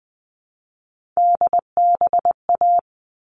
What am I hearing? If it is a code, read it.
Morse code, heard as a keyed tone, about 20 wpm, 700 Hz. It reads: DBA
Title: All Cut Numbers - 3-Digit 20wpm